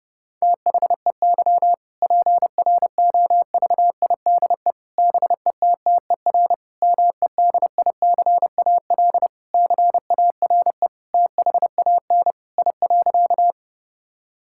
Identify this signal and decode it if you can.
Morse code, heard as a keyed tone, about 30 wpm, 700 Hz. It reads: THEY PROVIDE BETTER MEDICAL CARE THAN I.
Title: Sentences from Top 1000 Words - Review Part 5 - 30wpm